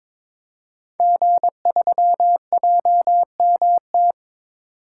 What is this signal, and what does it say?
Morse code, heard as a keyed tone, about 22 wpm, 700 Hz. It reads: G3JMT
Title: Code Groups - 5 Letter:Numbers - 22wpm